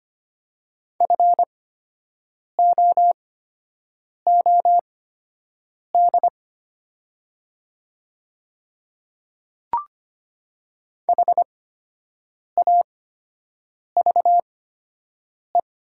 Morse code, heard as a keyed tone, about 25 wpm, spaced as for 7 wpm, 700 Hz. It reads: FOOD HAVE
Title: 8x Character Spacing - Top 500 Words - 25wpm